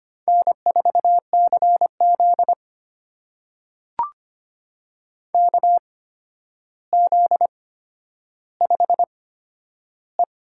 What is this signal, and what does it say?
Morse code, heard as a keyed tone, about 25 wpm, spaced as for 11 wpm, 700 Hz. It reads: N4CZ KZ5E
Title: USA Call Signs - 8x Character Spacing - 25wpm